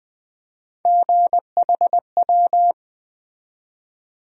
Morse code, heard as a keyed tone, about 21 wpm, 700 Hz. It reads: GHW
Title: USA Call Sign Postfixes - 3 Letters 20wpm